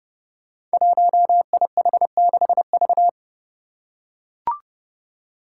Morse code, heard as a keyed tone, about 30 wpm, 700 Hz. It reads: 1IH6V
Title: Five Challenging Letter-Number Combinations 30wpm